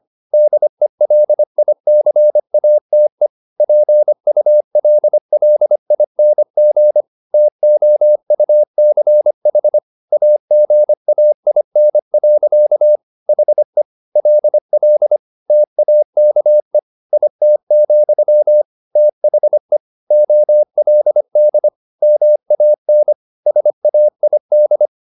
Morse code, heard as a keyed tone, about 25 wpm, 600 Hz. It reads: DELICATE PULLING TOUCH AGAIN. HE LL TAKE IT, THE OLD MAN SAID
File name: the old man and the sea part 1 25wpm